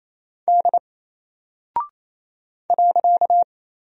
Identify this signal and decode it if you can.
Morse code, heard as a keyed tone, about 28 wpm, 700 Hz. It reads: D .
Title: Single Character (Letter:Number:Punctuation:Prosign) 28wpm